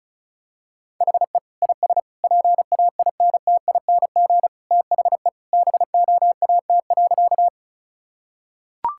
Morse code, heard as a keyed tone, about 34 wpm, 700 Hz. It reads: HE IS PAINTING THE BOAT.
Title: Sentences from Top 1000 Words - Review Part 6 - 35wpm